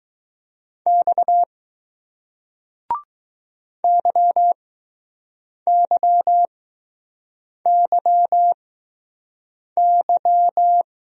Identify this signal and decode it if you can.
Morse code, heard as a keyed tone, about 20 wpm, spaced as for 6 wpm, 700 Hz. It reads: X YYYY